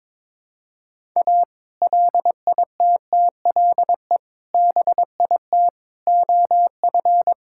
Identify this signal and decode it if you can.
Morse code, heard as a keyed tone, about 21 wpm, 700 Hz. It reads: A LITTLE BIT OF
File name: practice / 22wpm / Build A Phrase - 5-Word Phrases - 22wpm